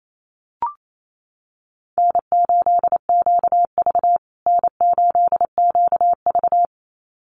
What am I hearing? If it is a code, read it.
Morse code, heard as a keyed tone, about 28 wpm, 700 Hz. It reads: N8QV N8QV
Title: USA Call Signs Repeated 2x 28wpm